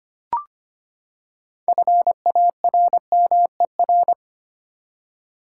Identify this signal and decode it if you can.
Morse code, heard as a keyed tone, about 25 wpm, 700 Hz. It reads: FARMER
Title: Six Letter Words - 25wpm